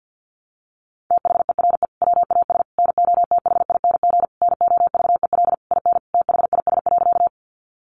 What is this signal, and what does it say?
Morse code, heard as a keyed tone, about 50 wpm, 700 Hz. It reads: THERE WAS NOTHING NOVEL IN THIS.
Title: Sentences from Top 1300 Words 50wpm